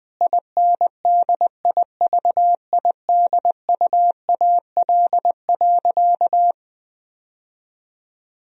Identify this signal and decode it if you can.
Morse code, heard as a keyed tone, about 21 wpm, 700 Hz. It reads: INDIVIDUAL.
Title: Sentences from Top 1000 Words - Review Part 4 - 20wpm